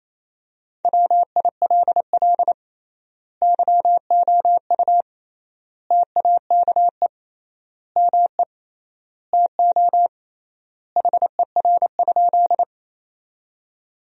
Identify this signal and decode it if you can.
Morse code, heard as a keyed tone, about 28 wpm, 700 Hz. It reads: WILL YOU TAKE ME TO HER?
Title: 3x Word-Spacing - Sentences from Top 200 Words - 28wpm